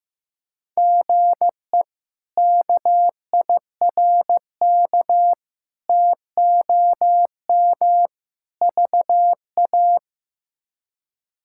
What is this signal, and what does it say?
Morse code, heard as a keyed tone, about 15 wpm, 700 Hz. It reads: GE KIRK TOM VA
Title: K1USN SST QSO Practice - 15wpm